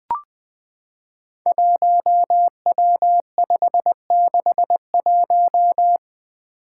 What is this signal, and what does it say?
Morse code, heard as a keyed tone, about 20 wpm, 700 Hz. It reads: 1W561
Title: Five Challenging Letter-Number Combinations 20wpm